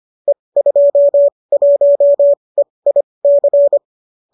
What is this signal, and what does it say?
Morse code, heard as a keyed tone, about 25 wpm, spaced as for 20 wpm, 590 Hz. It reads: E21EIC
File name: qso208 20